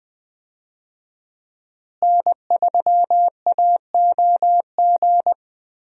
Morse code, heard as a keyed tone, about 21 wpm, 700 Hz. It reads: N3AOG
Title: USA Call Signs - 5 Characters - Repeated 2x - 20wpm